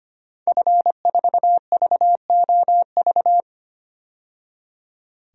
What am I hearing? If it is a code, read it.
Morse code, heard as a keyed tone, about 25 wpm, 700 Hz. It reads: F4VOV
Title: DXCC-100 Call Signs - 5 Characters - 25wpm